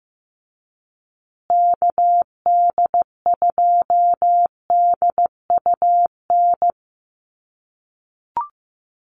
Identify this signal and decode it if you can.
Morse code, heard as a keyed tone, about 15 wpm, 700 Hz. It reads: KD2DUN
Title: USA Call Signs Repeated 2x 15wpm